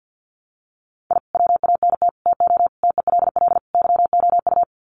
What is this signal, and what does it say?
Morse code, heard as a keyed tone, about 50 wpm, 700 Hz. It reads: I WANT TO TELL YOU
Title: Sets of 5 Words Encore 50wpm